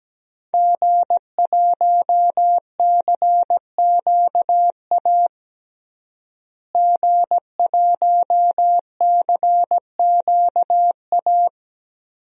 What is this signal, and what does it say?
Morse code, heard as a keyed tone, about 17 wpm, 700 Hz. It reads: G1CQA G1CQA
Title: DXCC-100 Call Signs - 5 Characters - Repeated 3x - 17wpm